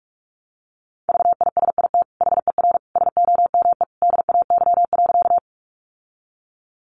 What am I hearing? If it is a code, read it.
Morse code, heard as a keyed tone, about 44 wpm, 700 Hz. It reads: VISIT HER SOME DAY.
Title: Sentences from Top 900 Words 45wpm